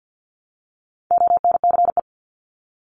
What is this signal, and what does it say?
Morse code, heard as a keyed tone, about 50 wpm, 700 Hz. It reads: ONCE
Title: Top 500 Words Repeated 50wpm